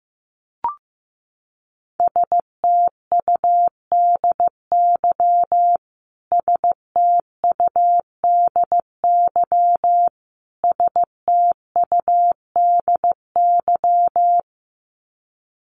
Morse code, heard as a keyed tone, about 15 wpm, 700 Hz. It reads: STUDY STUDY STUDY
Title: Top 300 Words Repeated 15wpm